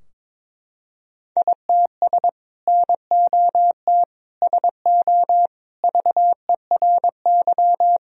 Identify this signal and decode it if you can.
Morse code, heard as a keyed tone, about 22 wpm, 700 Hz. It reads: ITS NOT SO VERY
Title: Sentences from Top 1000 Words - Review Part 3 - 22wpm